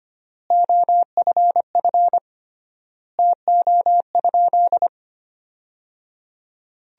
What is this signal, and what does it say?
Morse code, heard as a keyed tone, about 25 wpm, 700 Hz. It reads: OFF TO?
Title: 3x Word-Spacing - Sentences from Top 200 Words - 25wpm